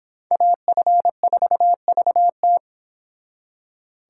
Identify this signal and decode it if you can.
Morse code, heard as a keyed tone, about 26 wpm, 700 Hz. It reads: AF4VT